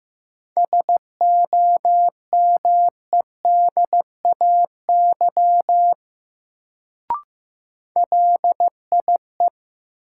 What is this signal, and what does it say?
Morse code, heard as a keyed tone, about 15 wpm, 700 Hz. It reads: SOMEDAY LIE
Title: Sentences from Top 1600 Words - New Word Drill - 15wpm